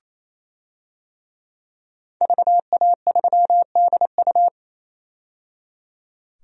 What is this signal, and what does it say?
Morse code, heard as a keyed tone, about 28 wpm, 700 Hz. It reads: VA3DU
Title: DXCC-100 Call Signs - 5 Characters - 28wpm